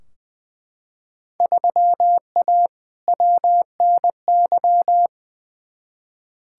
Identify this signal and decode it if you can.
Morse code, heard as a keyed tone, about 20 wpm, 700 Hz. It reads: 3A WNY
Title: Field Day Exchanges - 20wpm